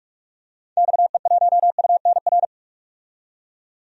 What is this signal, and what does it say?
Morse code, heard as a keyed tone, about 41 wpm, 700 Hz. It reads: XE1UNR
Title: DXCC-100 Call Signs - 6 Characters - 45wpm